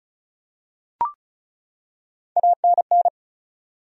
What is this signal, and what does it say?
Morse code, heard as a keyed tone, about 36 wpm, 700 Hz. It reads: ANN
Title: USA Common Names - Max 4 Letters - 35wpm